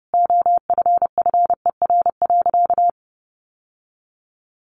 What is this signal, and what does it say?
Morse code, heard as a keyed tone, about 30 wpm, 700 Hz. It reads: OFFER.
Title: Sentences from Top 1100 Words 30wpm